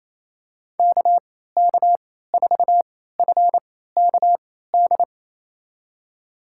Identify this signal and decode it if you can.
Morse code, heard as a keyed tone, about 28 wpm, spaced as for 16 wpm, 700 Hz. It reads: KK4FKD
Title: USA Call Signs - 6 Characters - 3x Character Spacing - 28wpm